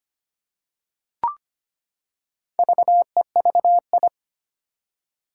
Morse code, heard as a keyed tone, about 25 wpm, 700 Hz. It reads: VEVI
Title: Four Challenging Letter-Number Combinations 25wpm